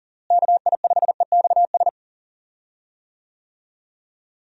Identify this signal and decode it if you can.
Morse code, heard as a keyed tone, about 37 wpm, 700 Hz. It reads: KI5EXS
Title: USA Call Signs - 6 Characters - Repeated 3x - 40wpm